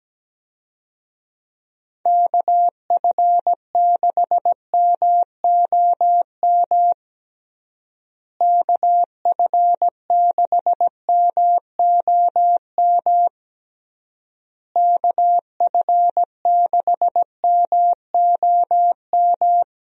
Morse code, heard as a keyed tone, about 17 wpm, 700 Hz. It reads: KF6MOM KF6MOM KF6MOM
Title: USA Call Signs - 6 Characters - Repeated 3x - 17wpm